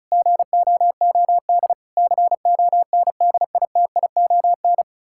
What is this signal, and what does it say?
Morse code, heard as a keyed tone, about 34 wpm, 700 Hz. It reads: GOOD CONDITION
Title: Pangram Challenge v2 - 35wpm